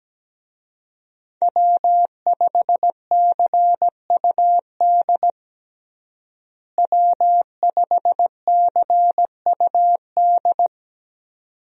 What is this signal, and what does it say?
Morse code, heard as a keyed tone, about 17 wpm, 700 Hz. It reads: W5CUD W5CUD
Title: USA Call Signs - 5 Characters - Repeated 2x - 17wpm